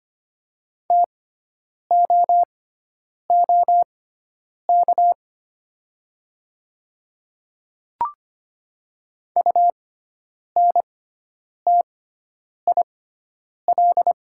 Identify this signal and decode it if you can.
Morse code, heard as a keyed tone, about 25 wpm, spaced as for 9 wpm, 700 Hz. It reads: TOOK UNTIL
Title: 6x Character Spacing - Top 500 Words - 25wpm